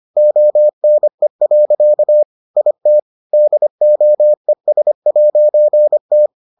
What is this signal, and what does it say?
Morse code, heard as a keyed tone, about 25 wpm, 600 Hz. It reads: ONE. IT DOES'T